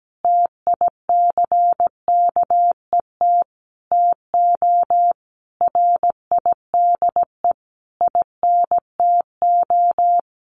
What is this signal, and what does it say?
Morse code, heard as a keyed tone, about 17 wpm, 700 Hz. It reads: TICKET TO RIDE INTO
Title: Sentences from Top 1400 Words 17wpm